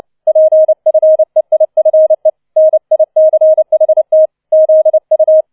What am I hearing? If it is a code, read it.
Morse code, heard as a keyed tone, about 29 wpm, 630 Hz. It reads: PFEIFE NICHT ZU